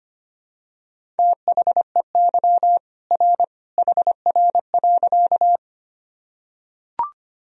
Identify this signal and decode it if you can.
Morse code, heard as a keyed tone, about 25 wpm, 700 Hz. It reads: THEY R HR.
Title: Sentences from Top 200 Words and Abbreviations 25wpm